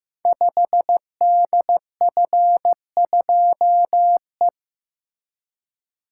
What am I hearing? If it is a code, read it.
Morse code, heard as a keyed tone, about 15 wpm, 700 Hz. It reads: 5DF2E
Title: Code Groups - 5 Letter:Numbers - 15wpm